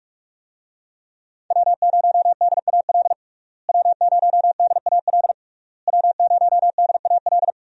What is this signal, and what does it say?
Morse code, heard as a keyed tone, about 41 wpm, 700 Hz. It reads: W0DAL W0DAL W0DAL
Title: USA Call Signs - 5 Characters - Repeated 3x - 45wpm